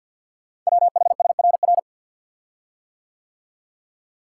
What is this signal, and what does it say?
Morse code, heard as a keyed tone, about 50 wpm, 700 Hz. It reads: WHSRR
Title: Code Groups - 5 Letters - 50wpm